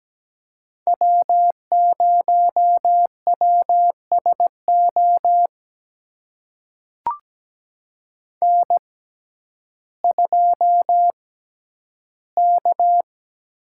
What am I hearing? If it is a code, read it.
Morse code, heard as a keyed tone, about 17 wpm, spaced as for 10 wpm, 700 Hz. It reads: W0WSO N2K
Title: USA Call Signs - 5 Characters - 6x Character Spacing - 17wpm